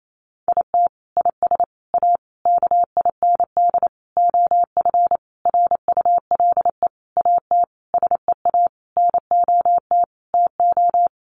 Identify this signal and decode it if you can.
Morse code, heard as a keyed tone, about 28 wpm, 700 Hz. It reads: IT IS A KIND OF RULE AT SEA NOT TO
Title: Sentences from Top 1500 Words 28wpm